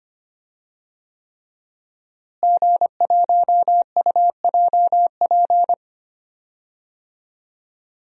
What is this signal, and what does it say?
Morse code, heard as a keyed tone, about 25 wpm, 700 Hz. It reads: G1UJP
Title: DXCC-100 Call Signs - 5 Characters - Repeated 3x - 25wpm